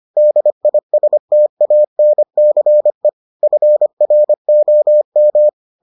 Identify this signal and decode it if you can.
Morse code, heard as a keyed tone, about 25 wpm, 600 Hz. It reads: DISTANCE FROM